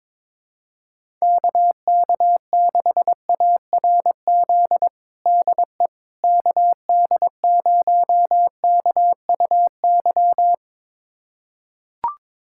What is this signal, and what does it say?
Morse code, heard as a keyed tone, about 21 wpm, 700 Hz. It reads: KK6ARZ DE KD0KUY
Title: USA Callsign DE Callsign 22wpm